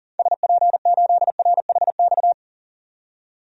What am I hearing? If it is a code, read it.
Morse code, heard as a keyed tone, about 40 wpm, 700 Hz. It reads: SP8RHX
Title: DXCC-100 Call Signs - 6 Characters - 40wpm